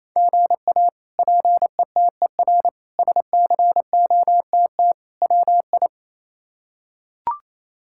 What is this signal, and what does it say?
Morse code, heard as a keyed tone, about 27 wpm, 700 Hz. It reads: GA PETER SCOTT WI
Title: K1USN SST QSO Practice - 28wpm